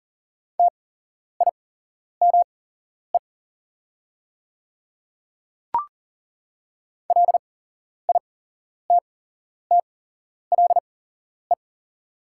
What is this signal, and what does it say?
Morse code, heard as a keyed tone, about 40 wpm, spaced as for 11 wpm, 700 Hz. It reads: TIME LITTLE
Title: 8x Character Spacing - Top 200 Words - 40wpm